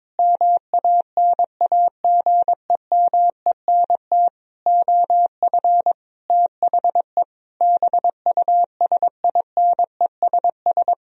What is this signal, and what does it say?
Morse code, heard as a keyed tone, about 22 wpm, 700 Hz. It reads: MANAGEMENT OF THE BUSINESS
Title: Sentences from Top 1000 Words - Review Part 4 - 22wpm